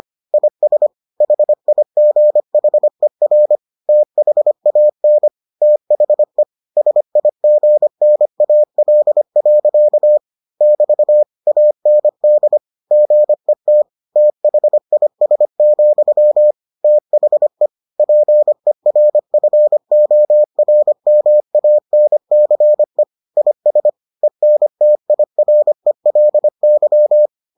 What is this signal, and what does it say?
Morse code, heard as a keyed tone, about 25 wpm, 600 Hz. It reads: IS HIGHER THAN THE SIGNAL. <BT> AND GET THIS, THE PERFORMANCE IS ENTIRELY